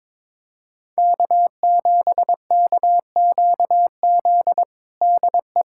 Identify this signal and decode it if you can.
Morse code, heard as a keyed tone, about 22 wpm, 700 Hz. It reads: K7KQZ DE
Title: USA Callsign DE Callsign 22wpm